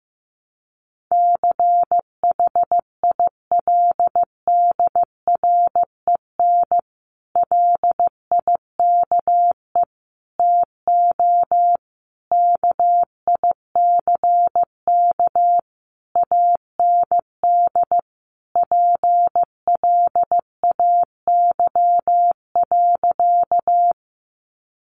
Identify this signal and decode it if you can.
Morse code, heard as a keyed tone, about 15 wpm, 700 Hz. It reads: CHILDREN LIKE TO KICK AND PLAY.
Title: Sentences from Top 1300 Words 15wpm